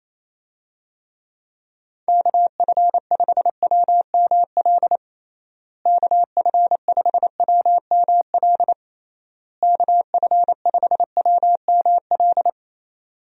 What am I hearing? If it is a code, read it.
Morse code, heard as a keyed tone, about 27 wpm, 700 Hz. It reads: KF5WML KF5WML KF5WML
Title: USA Call Signs - 6 Characters - Repeated 3x - 28wpm